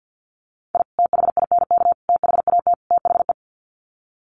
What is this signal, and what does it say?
Morse code, heard as a keyed tone, about 49 wpm, 700 Hz. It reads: I THINK THAT THE
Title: Sets of 4 Words 50wpm